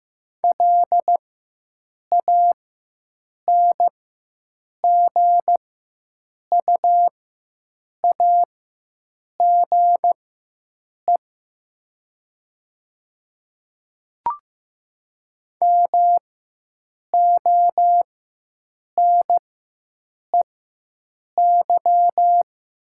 Morse code, heard as a keyed tone, about 15 wpm, spaced as for 7 wpm, 700 Hz. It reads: LANGUAGE MONEY
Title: 4x Character Spacing - Top 500 Words - 15wpm